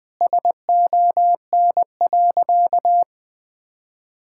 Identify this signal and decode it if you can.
Morse code, heard as a keyed tone, about 20 wpm, 700 Hz. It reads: SON.
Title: Sentences with 9-Word Phrases - 20wpm